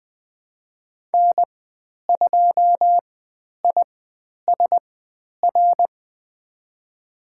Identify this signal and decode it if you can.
Morse code, heard as a keyed tone, about 20 wpm, spaced as for 10 wpm, 700 Hz. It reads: N2ISR
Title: 102 Lesson 17g - Letter u - Call Signs 20wpm